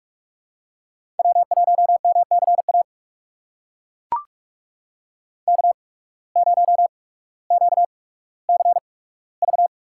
Morse code, heard as a keyed tone, about 42 wpm, spaced as for 19 wpm, 700 Hz. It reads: W1MCA K0QCV
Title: USA Call Signs - 5 Characters - 8x Character Spacing - 45wpm